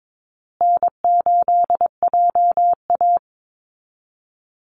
Morse code, heard as a keyed tone, about 22 wpm, 700 Hz. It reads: N8JA
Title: Easy Call Signs Repeated 3x - 22wpm